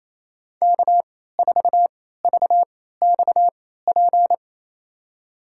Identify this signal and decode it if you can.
Morse code, heard as a keyed tone, about 28 wpm, spaced as for 16 wpm, 700 Hz. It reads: K4VXP
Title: USA Call Signs - 3x Character Spacing - 28wpm